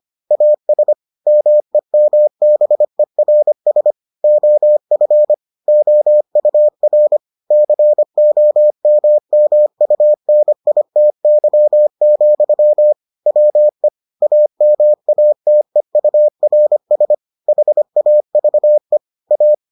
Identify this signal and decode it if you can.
Morse code, heard as a keyed tone, about 25 wpm, 600 Hz. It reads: AS MEMBERS OF OUR COMMUNITY, WE AMATEURS HAVE A